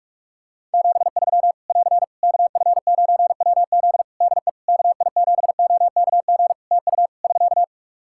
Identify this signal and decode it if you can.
Morse code, heard as a keyed tone, about 41 wpm, 700 Hz. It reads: 73 <AR> KF9WZ DE KI7OKG TU <SK>
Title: K1USN SST QSO Practice 45wpm